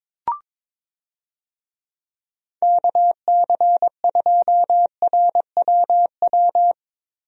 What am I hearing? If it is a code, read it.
Morse code, heard as a keyed tone, about 22 wpm, 700 Hz. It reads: KC2RWW
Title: USA Call Signs - 6 Characters - Repeated 2x - 22wpm